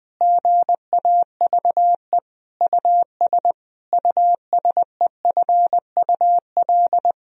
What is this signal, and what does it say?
Morse code, heard as a keyed tone, about 21 wpm, 700 Hz. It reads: GAVE US USEFUL
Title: Sentences from Top 1000 Words - Review Part 4 - 20wpm